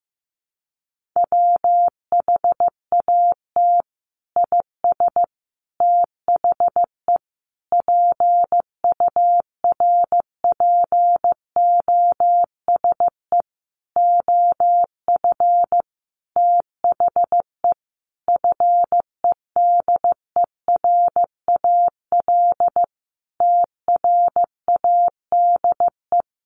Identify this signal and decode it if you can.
Morse code, heard as a keyed tone, about 15 wpm, 700 Hz. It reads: WHAT IS THE PURPOSE OF THE FEDERAL TRADE
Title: Sentences from Top 1400 Words 15wpm